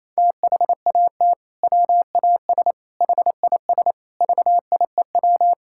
Morse code, heard as a keyed tone, about 28 wpm, 700 Hz. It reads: THAT WAS HIS VIEW
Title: Sentences from Top 1000 Words - Review Part 6 - 28wpm